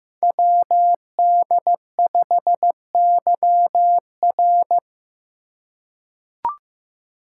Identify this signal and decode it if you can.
Morse code, heard as a keyed tone, about 15 wpm, 700 Hz. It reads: WD5YR
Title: Code Groups - 5 Letter:Numbers - 15wpm